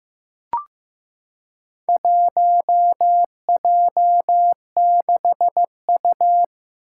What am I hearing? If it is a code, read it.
Morse code, heard as a keyed tone, about 15 wpm, 700 Hz. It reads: 1J6U
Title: Four Challenging Letter-Number Combinations 15wpm